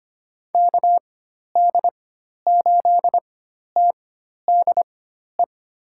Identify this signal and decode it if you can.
Morse code, heard as a keyed tone, about 25 wpm, spaced as for 12 wpm, 700 Hz. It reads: KD8TDE
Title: USA Call Signs - 6 Characters - 4x Character Spacing - 25wpm